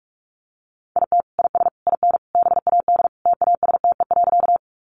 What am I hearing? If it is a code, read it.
Morse code, heard as a keyed tone, about 44 wpm, 700 Hz. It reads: IT IS IN BAD TASTE.
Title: Sentences from Top 1100 Words 45wpm